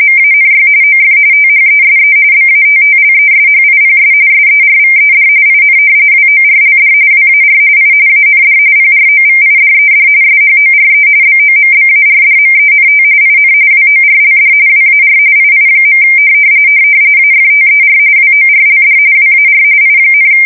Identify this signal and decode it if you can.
RTTY, about 45.45 baud. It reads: MANY OF THESE BATTLESHIPS SERVED IN WORLD WAR I AND PROVIDED SHORE BOMBARDMENT IN WORLD WAR II. AFTER WORLD WAR I,
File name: fsk170 / battleshipnj